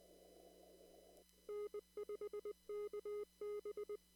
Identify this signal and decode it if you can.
Morse code, heard as a keyed tone, about 20 wpm, 420 Hz. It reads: N5KB
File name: CW / kt5j wpx cw-op-k5pi